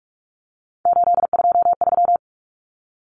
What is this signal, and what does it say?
Morse code, heard as a keyed tone, about 44 wpm, 700 Hz. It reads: 823
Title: Common Cut Numbers - 3-Digit 45wpm